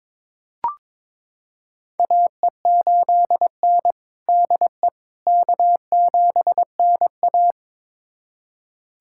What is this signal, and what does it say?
Morse code, heard as a keyed tone, about 22 wpm, 700 Hz. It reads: AE8N DE K7NA
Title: USA Callsign DE Callsign - 4 Characters - 22wpm